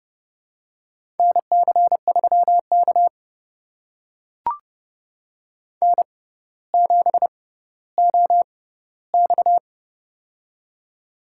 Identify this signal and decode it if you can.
Morse code, heard as a keyed tone, about 30 wpm, spaced as for 15 wpm, 700 Hz. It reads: NC3K N7OX
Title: USA Call Signs - 4 Characters - 6x Character Spacing - 30wpm